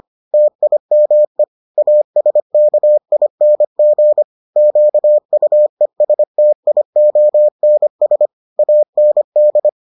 Morse code, heard as a keyed tone, about 25 wpm, 600 Hz. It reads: TIME ASKING QUESTIONS AND